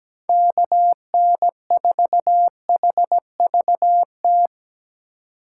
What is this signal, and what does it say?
Morse code, heard as a keyed tone, about 20 wpm, 700 Hz. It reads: KN4HVT
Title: Speed Racing - USA 2x3 Call Signs 17wpm